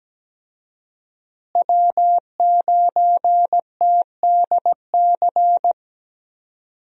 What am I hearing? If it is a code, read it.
Morse code, heard as a keyed tone, about 17 wpm, 700 Hz. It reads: W9TDC
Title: USA Call Signs - 5 Characters - Repeated 2x - 17wpm